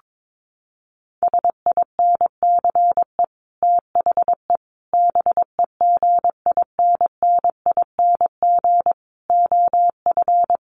Morse code, heard as a keyed tone, about 22 wpm, 700 Hz. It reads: SINCE THE BEGINNING OF
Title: Sets of 5 Words Encore2 22wpm